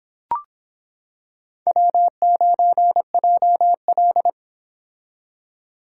Morse code, heard as a keyed tone, about 26 wpm, 700 Hz. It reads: W9JL